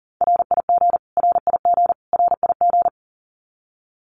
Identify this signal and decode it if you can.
Morse code, heard as a keyed tone, about 40 wpm, 700 Hz. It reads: RIG RIG RIG
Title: Abbreviations Repeated 3x - 40wpm